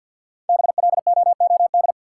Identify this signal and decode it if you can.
Morse code, heard as a keyed tone, about 53 wpm, 700 Hz. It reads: BLOOD
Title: Sentences from Top 1100 Words - New Word Drill Alt - 50wpm